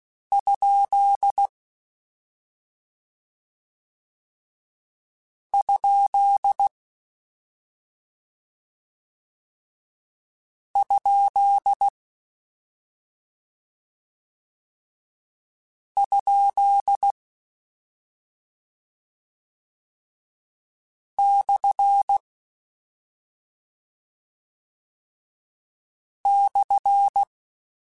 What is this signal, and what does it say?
Morse code, heard as a keyed tone, about 16 wpm, 800 Hz. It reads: ? ? ? ? / /
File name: Misc / Misc-mp3s / Punct @5